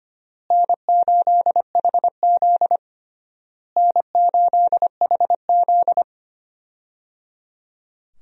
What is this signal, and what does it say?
Morse code, heard as a keyed tone, about 25 wpm, 700 Hz. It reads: N8HZ N8HZ
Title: USA Call Signs - 4 Characters - Repeated 3x - 25wpm